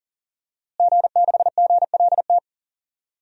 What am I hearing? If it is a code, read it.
Morse code, heard as a keyed tone, about 40 wpm, 700 Hz. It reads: G6GLT
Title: DXCC-100 Call Signs - 5 Characters - Repeated 2x - 40wpm